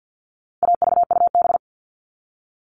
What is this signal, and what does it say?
Morse code, heard as a keyed tone, about 50 wpm, 700 Hz. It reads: AVUB
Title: All Cut Numbers - 4-Digit 50wpm